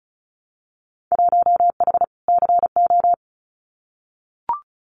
Morse code, heard as a keyed tone, about 36 wpm, 700 Hz. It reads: 1H CO
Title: Winter Field Day Exchanges 35wpm